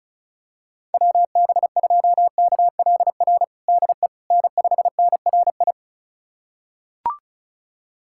Morse code, heard as a keyed tone, about 34 wpm, 700 Hz. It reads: WB2KLR DE N5NRI
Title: USA Callsign DE Callsign 35wpm